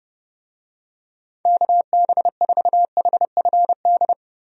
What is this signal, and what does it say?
Morse code, heard as a keyed tone, about 30 wpm, 700 Hz. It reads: KB4HFD
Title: USA Call Signs - 6 Characters - 30wpm